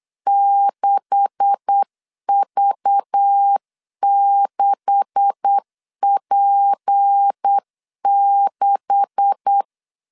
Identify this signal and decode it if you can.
Morse code, heard as a keyed tone, about 8 wpm, 800 Hz. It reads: 6V6P6
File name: LEZIONI / 9 286 8wpm 3425